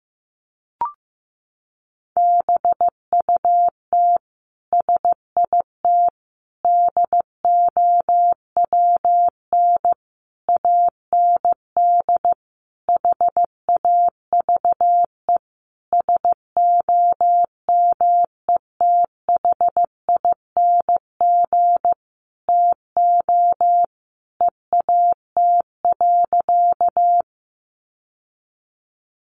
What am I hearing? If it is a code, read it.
Morse code, heard as a keyed tone, about 15 wpm, 700 Hz. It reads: BUT SIT DOWN AND HAVE SOMETHING TO EAT.
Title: Sentences from Top 600 Words 15wpm